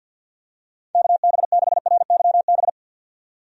Morse code, heard as a keyed tone, about 50 wpm, 700 Hz. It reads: KB6RYB
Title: USA Call Signs - 6 Characters - 50wpm